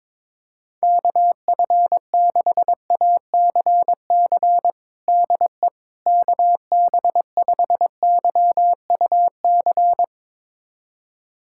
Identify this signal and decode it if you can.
Morse code, heard as a keyed tone, about 21 wpm, 700 Hz. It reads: KF6ACC DE KB5YUC
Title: USA Callsign DE Callsign 22wpm